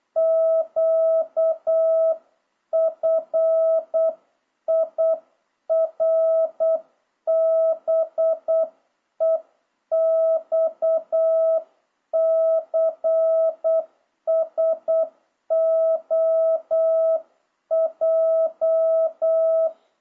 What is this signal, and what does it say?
Morse code, heard as a keyed tone, about 8 wpm, 650 Hz. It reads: QFIRBEXCSOJ
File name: jh8cbh / 2026cw